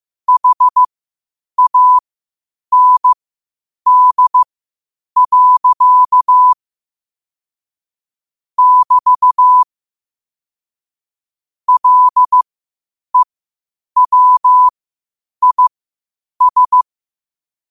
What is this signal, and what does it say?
Morse code, heard as a keyed tone, about 15 wpm, spaced as for 8 wpm, 1000 Hz. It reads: HAND. <BT> LEWIS
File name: qotdfiles / 8wpm